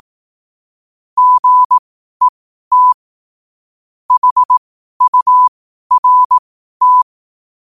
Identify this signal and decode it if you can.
Morse code, heard as a keyed tone, about 18 wpm, spaced as for 12 wpm, 1000 Hz. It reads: GET HURT
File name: qotdfiles / 12wpm